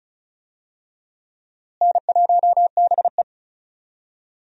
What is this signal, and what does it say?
Morse code, heard as a keyed tone, about 34 wpm, 700 Hz. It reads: N1BE
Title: USA Call Signs - 4 Characters - Repeated 3x - 35wpm